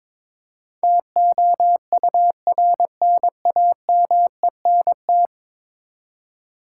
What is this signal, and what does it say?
Morse code, heard as a keyed tone, about 22 wpm, 700 Hz. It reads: TOURNAMENT
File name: practice / 22wpm / 10 Letter Words - First 3 Letters Given - 22wpm